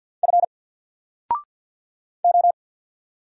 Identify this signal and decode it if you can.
Morse code, heard as a keyed tone, about 50 wpm, 700 Hz. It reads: F O
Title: Single Letter-Number Alt 50wpm